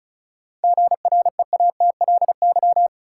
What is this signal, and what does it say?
Morse code, heard as a keyed tone, about 34 wpm, 700 Hz. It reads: GREATLY
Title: Sentences from Top 1400 Words - New Word Drill - 35wpm